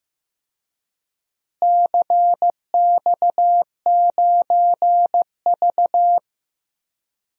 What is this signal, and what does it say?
Morse code, heard as a keyed tone, about 15 wpm, 700 Hz. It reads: CX9V
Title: DXCC-100 Call Signs - 4 Characters - Repeated 2x - 15wpm